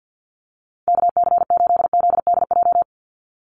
Morse code, heard as a keyed tone, about 50 wpm, 700 Hz. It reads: KC8ZDJ
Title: USA Call Signs 50wpm